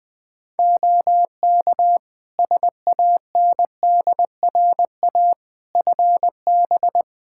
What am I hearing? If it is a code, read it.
Morse code, heard as a keyed tone, about 20 wpm, 700 Hz. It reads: OK SANDRA FB
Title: Practice QSO - Exchange 2 - Info - 20wpm